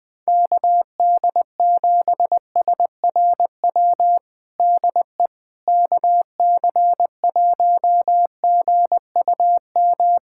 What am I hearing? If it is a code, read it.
Morse code, heard as a keyed tone, about 20 wpm, 700 Hz. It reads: KD7SRW DE KC1GUM
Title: USA Callsign DE Callsign - 6 Characters - 20wpm